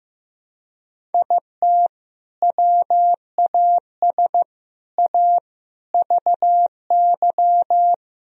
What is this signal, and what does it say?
Morse code, heard as a keyed tone, about 15 wpm, 700 Hz. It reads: IT WAS A VY
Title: Sentences from Top 500 Words and Abbreviations 15wpm